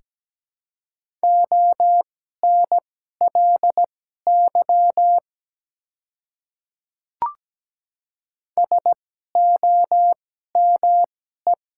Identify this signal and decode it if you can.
Morse code, heard as a keyed tone, about 17 wpm, spaced as for 12 wpm, 690 Hz. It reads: ONLY SOME
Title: 2x Character Spacing - Top 300 Words - 17wpm